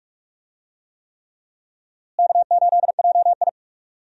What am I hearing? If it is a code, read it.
Morse code, heard as a keyed tone, about 43 wpm, 700 Hz. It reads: K8JI
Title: USA Call Signs - 4 Characters - Repeated 3x - 45wpm